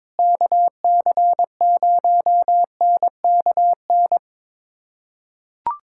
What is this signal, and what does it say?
Morse code, heard as a keyed tone, about 22 wpm, 700 Hz. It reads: KC0NKN
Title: USA Callsign DE Callsign 22wpm